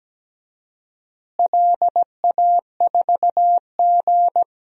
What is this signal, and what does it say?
Morse code, heard as a keyed tone, about 17 wpm, 700 Hz. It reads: LA4G
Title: DXCC-100 Call Signs - 4 Characters - Repeated 2x - 17wpm